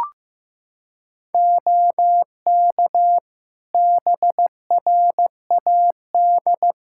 Morse code, heard as a keyed tone, about 16 wpm, 700 Hz. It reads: OK BRAD
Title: Practice QSO - Exchange 3 - More Info - 15wpm